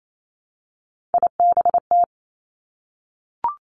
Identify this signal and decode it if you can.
Morse code, heard as a keyed tone, about 28 wpm, 700 Hz. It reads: IBT
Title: Three Challenging Letter-Number Combinations 28wpm